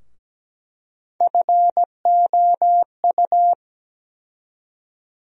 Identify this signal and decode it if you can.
Morse code, heard as a keyed tone, about 17 wpm, 700 Hz. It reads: FOU
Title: USA Call Sign Postfixes - 3 Letters 17wpm